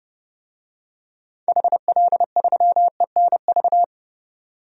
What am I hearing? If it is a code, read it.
Morse code, heard as a keyed tone, about 30 wpm, 700 Hz. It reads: HL3ENV
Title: DXCC-100 Call Signs - 6 Characters - 30wpm